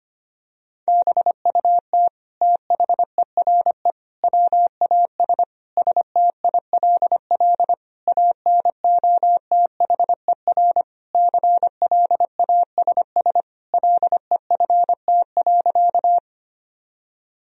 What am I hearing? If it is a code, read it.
Morse code, heard as a keyed tone, about 25 wpm, 700 Hz. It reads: BUT THERE WAS STILL ANOTHER CLASS LEFT.
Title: Sentences from Top 1000 Words - Review Part 2 - 25wpm